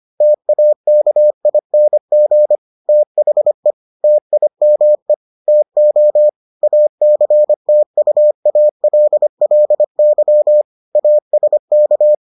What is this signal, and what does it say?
Morse code, heard as a keyed tone, about 25 wpm, 600 Hz. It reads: TAKING THE TIME TO ACTUALLY ASK